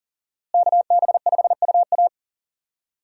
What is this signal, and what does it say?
Morse code, heard as a keyed tone, about 40 wpm, 700 Hz. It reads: KB5UA
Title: USA Call Signs - 5 Characters - Repeated 3x - 40wpm